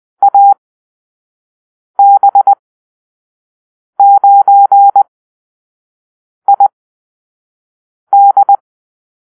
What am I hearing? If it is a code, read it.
Morse code, heard as a keyed tone, about 20 wpm, spaced as for 5 wpm, 800 Hz. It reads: AB9ID
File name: arrl / Call-Sign Drill-1